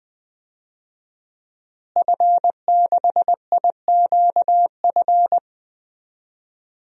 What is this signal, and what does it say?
Morse code, heard as a keyed tone, about 20 wpm, 700 Hz. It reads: F6IQF
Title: DXCC-100 Call Signs - 5 Characters - Repeated 2x - 20wpm